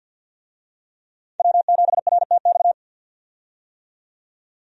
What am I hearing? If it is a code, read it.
Morse code, heard as a keyed tone, about 51 wpm, 700 Hz. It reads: W7RTX
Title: USA Call Signs - 5 Characters - 50wpm